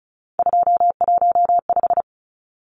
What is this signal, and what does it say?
Morse code, heard as a keyed tone, about 34 wpm, 700 Hz. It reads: 215
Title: Numbers - 3-Digit 35wpm